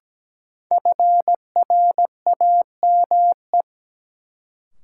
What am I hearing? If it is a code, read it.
Morse code, heard as a keyed tone, about 17 wpm, 700 Hz. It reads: FRAME
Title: Sentences from Top 1000 Words - Review Part 6 - New Word Drill - 17wpm